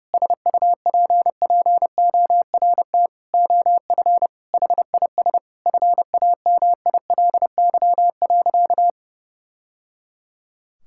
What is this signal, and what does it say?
Morse code, heard as a keyed tone, about 30 wpm, 700 Hz. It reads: SUPPORT OF HIS FAMILY.
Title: Sentences from Top 1000 Words - Review Part 5 - 30wpm